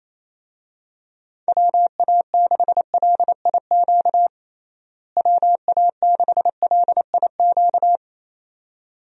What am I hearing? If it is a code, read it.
Morse code, heard as a keyed tone, about 28 wpm, 700 Hz. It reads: WA6LIQ WA6LIQ
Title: USA Call Signs - 6 Characters - Repeated 2x - 28wpm